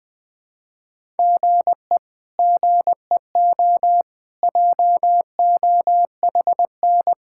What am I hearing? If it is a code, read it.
Morse code, heard as a keyed tone, about 20 wpm, 700 Hz. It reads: GE GEO JOHN
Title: K1USN SST QSO Practice - 20wpm